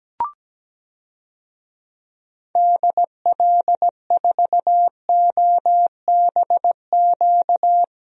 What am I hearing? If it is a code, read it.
Morse code, heard as a keyed tone, about 17 wpm, 700 Hz. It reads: DL4OBQ
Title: USA Call Signs - 6 Characters - Repeated 3x - 17wpm